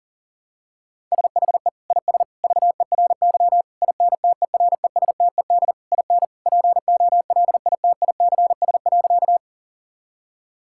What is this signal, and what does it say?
Morse code, heard as a keyed tone, about 40 wpm, 700 Hz. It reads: SHE IS VERY INTERESTED IN POLITICS.
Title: Sentences from Top 1000 Words - Review Part 6 - 40wpm